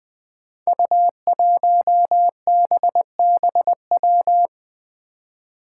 Morse code, heard as a keyed tone, about 20 wpm, 700 Hz. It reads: U1BBW
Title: Five Challenging Letter-Number Combinations 20wpm